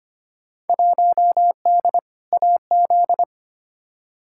Morse code, heard as a keyed tone, about 25 wpm, 700 Hz. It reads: 1D AZ
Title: Field Day Exchanges - 25wpm